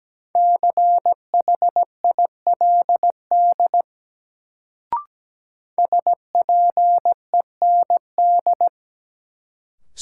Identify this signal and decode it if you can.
Morse code, heard as a keyed tone, about 17 wpm, 700 Hz. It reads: CHILD SPEND
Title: Sentences from Top 800 Words - New Word Drill - 17wpm